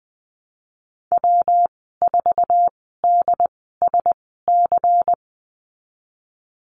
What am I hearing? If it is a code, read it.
Morse code, heard as a keyed tone, about 20 wpm, spaced as for 14 wpm, 700 Hz. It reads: W4DSC
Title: 161 Lesson 26g - Letter g - Call Signs 20wpm